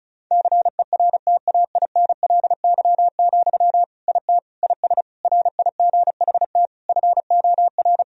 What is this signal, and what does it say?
Morse code, heard as a keyed tone, about 34 wpm, 700 Hz. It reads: CERTAINLY, IT IS RIGHT FOR
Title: Sentences from Top 1000 Words - Review Part 2 - 35wpm